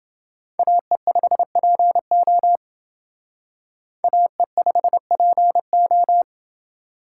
Morse code, heard as a keyed tone, about 29 wpm, 700 Hz. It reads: AE5PO AE5PO